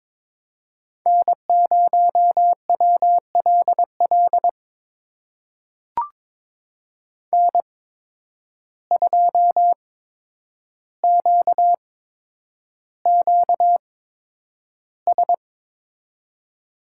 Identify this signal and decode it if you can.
Morse code, heard as a keyed tone, about 22 wpm, spaced as for 9 wpm, 700 Hz. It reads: N0WLL N2QQS
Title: USA Call Signs - 5 Characters - 8x Character Spacing - 22wpm